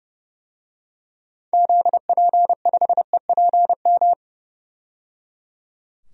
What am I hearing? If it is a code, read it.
Morse code, heard as a keyed tone, about 30 wpm, 700 Hz. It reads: ZP5EPM
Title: DXCC-100 Call Signs - 6 Characters - 30wpm